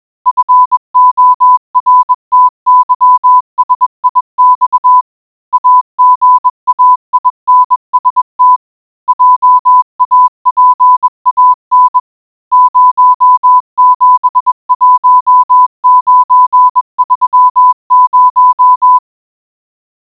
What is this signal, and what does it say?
Morse code, heard as a keyed tone, about 21 wpm, 1000 Hz. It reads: FORTYSIX AGAINST JAPAN 071930